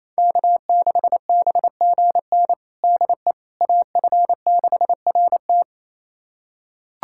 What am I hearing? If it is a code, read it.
Morse code, heard as a keyed tone, about 28 wpm, 700 Hz. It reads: K6BGN DE AF6RT
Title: USA Callsign DE Callsign - 5 Characters - 28wpm